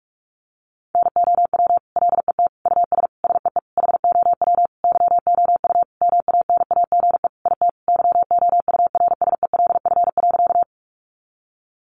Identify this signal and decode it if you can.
Morse code, heard as a keyed tone, about 44 wpm, 700 Hz. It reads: NOW LET US SEE HOW YOU MANAGE IT YOURSELF.
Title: Sentences from Top 1400 Words 45wpm